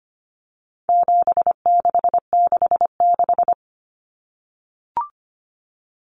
Morse code, heard as a keyed tone, about 25 wpm, 700 Hz. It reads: 7666
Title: Numbers - 4-Digit 25wpm